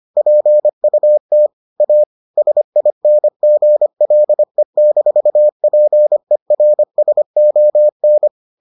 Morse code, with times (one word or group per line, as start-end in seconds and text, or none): 0.2-1.5 PUT
1.8-2.0 A
2.4-8.3 SINGLE-PERSON